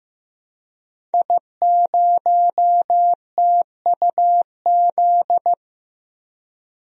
Morse code, heard as a keyed tone, about 15 wpm, 700 Hz. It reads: I0TUZ
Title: DXCC-100 Call Signs - 5 Characters - Repeated 3x - 15wpm